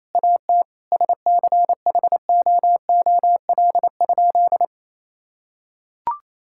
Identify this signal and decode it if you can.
Morse code, heard as a keyed tone, about 27 wpm, 700 Hz. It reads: AT SCHOOL?
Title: Sentences from Top 1000 Words - Review Part 6 - 28wpm